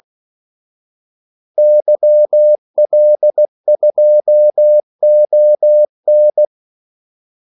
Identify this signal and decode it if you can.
Morse code, heard as a keyed tone, about 16 wpm, 600 Hz. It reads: YL2ON
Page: Kaldesignaler 16 wpm